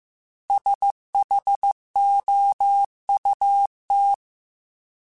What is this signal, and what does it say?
Morse code, heard as a keyed tone, about 15 wpm, 800 Hz. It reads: SHOUT
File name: S vs H / S vs H mp3s / S vs H @13